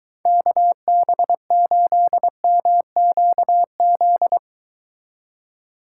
Morse code, heard as a keyed tone, about 22 wpm, 700 Hz. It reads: KB8MQZ